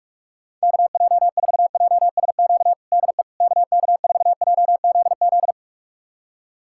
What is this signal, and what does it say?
Morse code, heard as a keyed tone, about 46 wpm, 700 Hz. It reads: KJ4JSQ DE KK4JZZ
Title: USA Callsign DE Callsign 45wpm